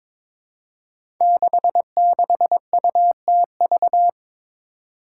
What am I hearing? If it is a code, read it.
Morse code, heard as a keyed tone, about 22 wpm, 700 Hz. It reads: 66UTV
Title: Five Challenging Letter-Number Combinations 22wpm